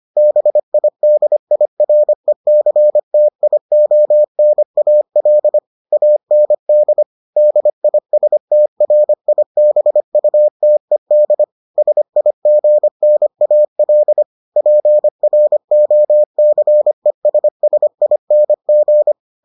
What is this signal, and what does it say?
Morse code, heard as a keyed tone, about 25 wpm, 600 Hz. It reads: BIDIRECTIONAL AND DISTRIBUTED SIGNAL PROCESSING